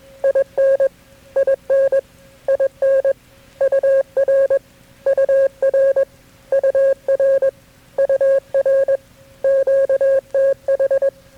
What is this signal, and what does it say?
Morse code, heard as a keyed tone, about 21 wpm, 540 Hz. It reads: IN IN IN UR UR UR UR QTH
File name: KB6MT-tape3-side1